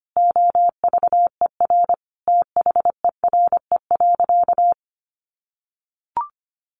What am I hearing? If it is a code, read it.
Morse code, heard as a keyed tone, about 25 wpm, 700 Hz. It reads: OVER THERE.
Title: Sentences from Top 800 Words 25wpm